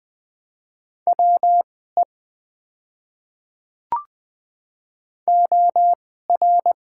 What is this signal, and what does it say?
Morse code, heard as a keyed tone, about 20 wpm, spaced as for 14 wpm, 700 Hz. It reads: WE OR
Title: 2x Character Spacing - Top 300 Words - 20wpm